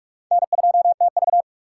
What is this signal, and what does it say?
Morse code, heard as a keyed tone, about 45 wpm, 700 Hz. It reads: N2TV
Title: USA Call Signs - 4 Characters - 2x Character Spacing - 45wpm